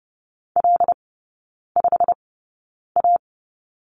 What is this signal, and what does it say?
Morse code, heard as a keyed tone, about 30 wpm, spaced as for 9 wpm, 700 Hz. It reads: L5A
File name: Lesson 16e - Letter c - Three Characters 30wpm